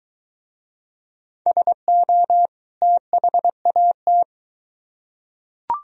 Morse code, heard as a keyed tone, about 20 wpm, 700 Hz. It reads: SO THAT